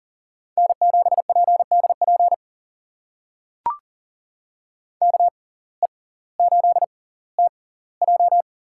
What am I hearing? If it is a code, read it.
Morse code, heard as a keyed tone, about 40 wpm, spaced as for 20 wpm, 700 Hz. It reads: N7PDP KE8TJ
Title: USA Call Signs - 5 Characters - 6x Character Spacing - 40wpm